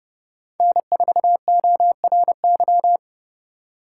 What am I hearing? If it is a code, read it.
Morse code, heard as a keyed tone, about 29 wpm, 700 Hz. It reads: N4ORY